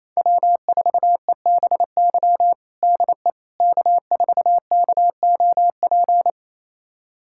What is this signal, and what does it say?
Morse code, heard as a keyed tone, about 28 wpm, 700 Hz. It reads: W4EBY DE K4KOP
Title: USA Callsign DE Callsign - 5 Characters - 28wpm